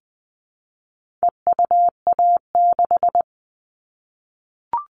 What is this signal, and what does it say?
Morse code, heard as a keyed tone, about 20 wpm, 700 Hz. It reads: EUA6
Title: All Cut Numbers - 4-Digit 20wpm